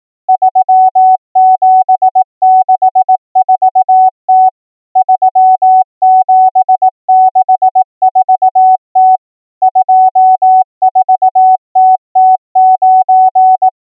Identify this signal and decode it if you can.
Morse code, heard as a keyed tone, about 18 wpm, 750 Hz. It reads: 3764T 3764T 24TT9
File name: numbers / V2-morse